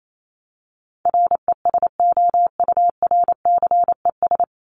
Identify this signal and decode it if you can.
Morse code, heard as a keyed tone, about 28 wpm, 700 Hz. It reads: RESOURCES
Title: ES:S Suffix - Top 100 Words with Base Words 28wpm